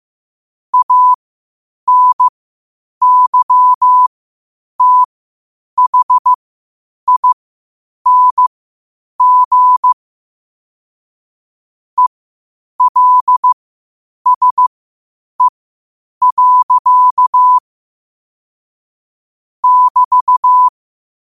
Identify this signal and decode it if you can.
Morse code, heard as a keyed tone, about 15 wpm, spaced as for 8 wpm, 1000 Hz. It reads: ANYTHING ELSE. <BT>
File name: qotdfiles / 8wpm